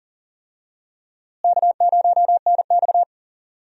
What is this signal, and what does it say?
Morse code, heard as a keyed tone, about 40 wpm, 700 Hz. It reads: K0NX
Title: USA Call Signs - 4 Characters - Repeated 2x - 40wpm